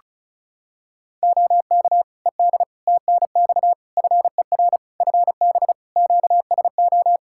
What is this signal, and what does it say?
Morse code, heard as a keyed tone, about 34 wpm, 700 Hz. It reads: OK ED TNX FER FB QSO
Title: K1USN SST QSO Practice 35wpm